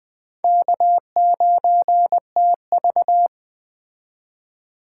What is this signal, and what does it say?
Morse code, heard as a keyed tone, about 20 wpm, 700 Hz. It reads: K9TV
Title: Speed Racing - USA 1x2 Call Signs 17wpm